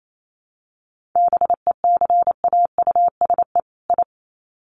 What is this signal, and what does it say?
Morse code, heard as a keyed tone, about 28 wpm, 700 Hz. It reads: BECAUSE I
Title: Sets of 2 Words 28wpm